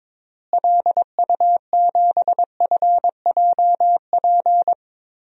Morse code, heard as a keyed tone, about 22 wpm, 700 Hz. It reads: LU7FJP
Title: DXCC-100 Call Signs - 6 Characters - Repeated 3x - 22wpm